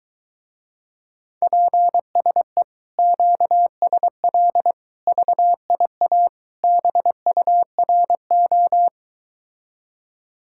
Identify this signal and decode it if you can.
Morse code, heard as a keyed tone, about 22 wpm, 700 Hz. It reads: PSE QSL VIA BURO